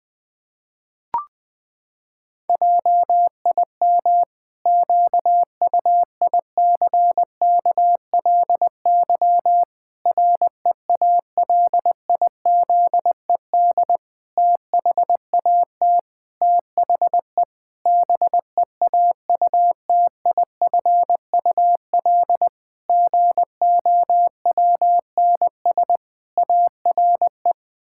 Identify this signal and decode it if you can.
Morse code, heard as a keyed tone, about 20 wpm, 700 Hz. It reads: JIM QUICKLY REALIZED THAT THE BEAUTIFUL GOWNS ARE
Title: Pangram Challenge v2 - 20wpm